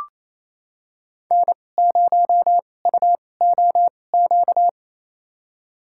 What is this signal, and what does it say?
Morse code, heard as a keyed tone, about 28 wpm, spaced as for 20 wpm, 700 Hz. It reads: N0UOQ
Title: USA Call Signs - 5 Characters - 2x Character Spacing - 28wpm